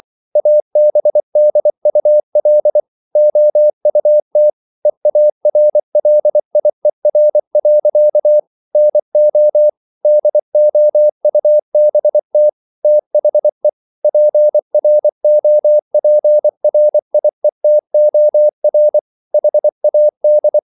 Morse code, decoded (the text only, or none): ABDUL OUT EARLIER. NO DOUBT THE PROPRIETOR HAD